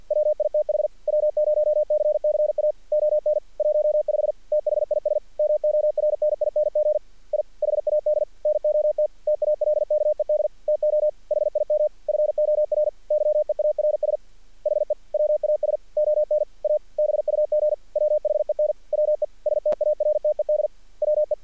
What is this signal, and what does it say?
Morse code, heard as a keyed tone, about 50 wpm, 600 Hz. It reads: WITH W0CCA ON 15 THIS MORNING I HAD NOT TALKED TO HIM FOR YEARS HE WAS ON A BUG WHEN WE STARTED WE
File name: archived / CRSnet